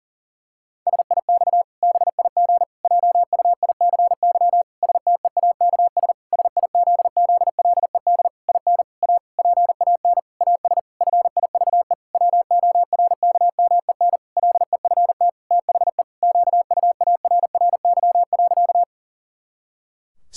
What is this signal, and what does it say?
Morse code, heard as a keyed tone, about 40 wpm, 700 Hz. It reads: SIX BIG JUICY STEAKS SIZZLED IN A PAN AS FIVE WORKMEN LEFT THE QUARRY.
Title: Pangram Challenge v2 - 40wpm